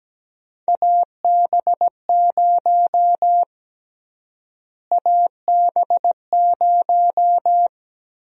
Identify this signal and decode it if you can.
Morse code, heard as a keyed tone, about 17 wpm, 700 Hz. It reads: AB0 AB0
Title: USA Call Sign Prefixes - Repeated 2x - 17wpm